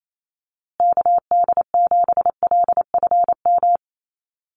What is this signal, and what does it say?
Morse code, heard as a keyed tone, about 28 wpm, 700 Hz. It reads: KD7LFM
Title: USA Call Signs Repeated 2x 28wpm